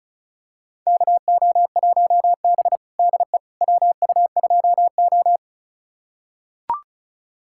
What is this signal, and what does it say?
Morse code, heard as a keyed tone, about 36 wpm, 700 Hz. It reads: KO1B DE WU2O
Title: USA Callsign DE Callsign - 4 Characters - 35wpm